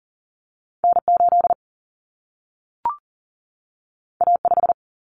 Morse code, heard as a keyed tone, about 40 wpm, 700 Hz. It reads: N 8 A 5
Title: Common Cut Numbers - 2-Digit 40wpm